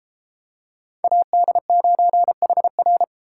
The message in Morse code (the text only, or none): AD9HR